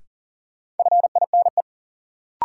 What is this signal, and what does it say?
Morse code, heard as a keyed tone, about 40 wpm, 700 Hz. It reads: FINE
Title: Sentences from Top 500 Words - New Word Drill - 40wpm